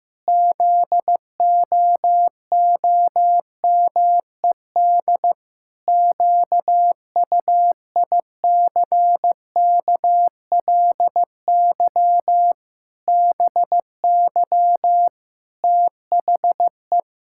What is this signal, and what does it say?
Morse code, heard as a keyed tone, about 15 wpm, 700 Hz. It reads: ZOOMED QUICKLY BY THE
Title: Pangram Challenge v2 - 15wpm